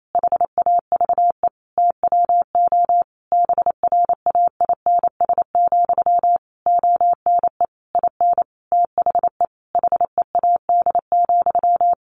Morse code, heard as a keyed tone, about 28 wpm, 700 Hz. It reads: HAVE TWO BRAINS, ONE IN THE HEAD,
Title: A Sound Of Thunder 28wpm